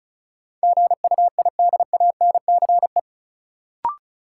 Morse code, heard as a keyed tone, about 36 wpm, 700 Hz. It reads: GUIDANCE
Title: Sentences from Top 1600 Words - New Word Drill - 35wpm